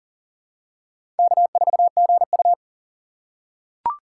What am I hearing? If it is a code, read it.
Morse code, heard as a keyed tone, about 40 wpm, 700 Hz. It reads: K4GU
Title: USA Call Signs - 4 Characters - 6x Character Spacing - 40wpm